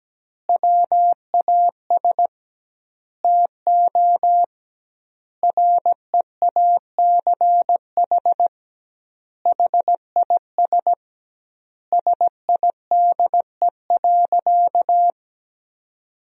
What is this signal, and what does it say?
Morse code, heard as a keyed tone, about 17 wpm, 700 Hz. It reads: WAS TO REACH HIS SIDE.
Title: 2x Word-Spacing - Sentences from Top 400 Words - 17wpm